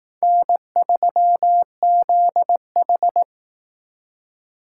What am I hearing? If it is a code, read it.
Morse code, heard as a keyed tone, about 18 wpm, 700 Hz. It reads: N3ZH
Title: Speed Racing - USA Call Signs 15wpm